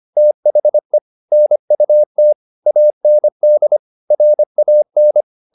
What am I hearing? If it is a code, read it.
Morse code, heard as a keyed tone, about 25 wpm, 600 Hz. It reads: THE NUT AND RAN